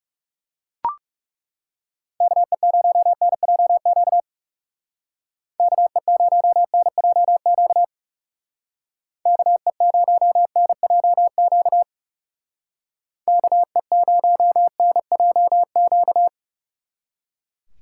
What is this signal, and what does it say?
Morse code, heard as a keyed tone, about 35 wpm, 700 Hz. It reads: KE0NJQ KE0NJQ KE0NJQ KE0NJQ
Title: Speed Racing - USA 2x3 Call Signs 30wpm